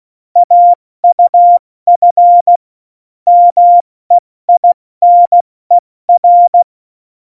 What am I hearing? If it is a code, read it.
Morse code, heard as a keyed tone, about 16 wpm, 700 Hz. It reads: AUF MEINER
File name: Morsekurs / Witze / Witz 26. Januar 2021, 70 F80